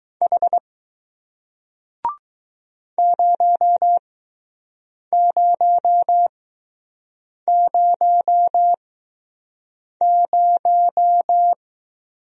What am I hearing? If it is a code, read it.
Morse code, heard as a keyed tone, about 21 wpm, spaced as for 6 wpm, 700 Hz. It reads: H 0000